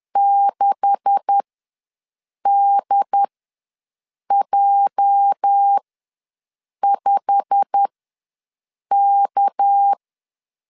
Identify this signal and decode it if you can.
Morse code, heard as a keyed tone, about 10 wpm, spaced as for 6 wpm, 800 Hz. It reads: 6DJ5K
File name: LEZIONI / 10 SIGNS / 10A SLASH ws 10wpm 4121